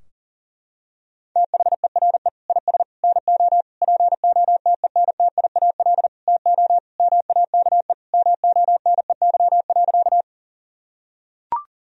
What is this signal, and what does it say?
Morse code, heard as a keyed tone, about 40 wpm, 700 Hz. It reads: THERE IS NO POTENTIAL TO MAKE MONEY.
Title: Sentences from Top 1000 Words - Review Part 4 - 40wpm